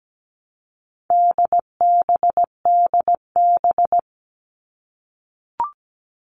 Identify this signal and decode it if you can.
Morse code, heard as a keyed tone, about 17 wpm, 700 Hz. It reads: DBDB
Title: All Cut Numbers - 4-Digit 17wpm